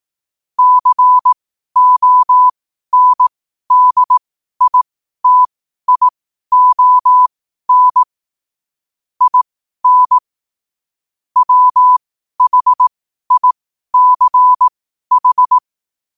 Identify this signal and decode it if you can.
Morse code, heard as a keyed tone, about 18 wpm, spaced as for 12 wpm, 1000 Hz. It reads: CONDITION IN WHICH
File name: qotdfiles / 12wpm